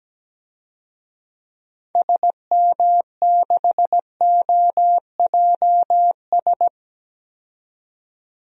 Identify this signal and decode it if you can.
Morse code, heard as a keyed tone, about 17 wpm, 700 Hz. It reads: SM6OJS
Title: DXCC-100 Call Signs - 6 Characters - 17wpm